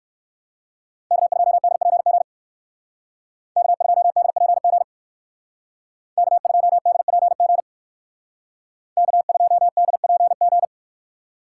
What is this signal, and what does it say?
Morse code, heard as a keyed tone, about 58 wpm, 700 Hz. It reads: K2DPG K2DPG K2DPG K2DPG